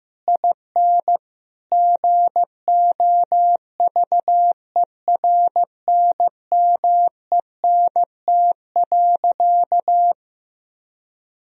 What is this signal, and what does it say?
Morse code, heard as a keyed tone, about 15 wpm, 700 Hz. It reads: IN GOVERNMENT.
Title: Sentences from Top 1000 Words - Review Part 4 - 15wpm